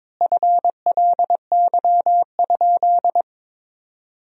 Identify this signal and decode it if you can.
Morse code, heard as a keyed tone, about 22 wpm, 700 Hz. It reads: FLY?
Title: 2x Word-Spacing - Sentences from Top 500 Words - 22wpm